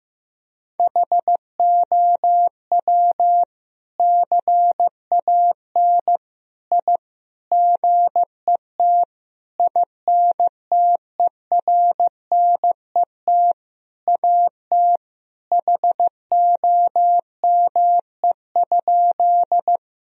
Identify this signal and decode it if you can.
Morse code, heard as a keyed tone, about 15 wpm, 700 Hz. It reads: HOW CAN I GET INTERNET AT HOME?
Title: Sentences from Top 1000 Words - Review Part 4 - 15wpm